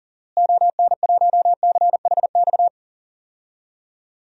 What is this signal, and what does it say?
Morse code, heard as a keyed tone, about 40 wpm, 700 Hz. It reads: ON1CHX
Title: DXCC-100 Call Signs - 6 Characters - Repeated 2x - 40wpm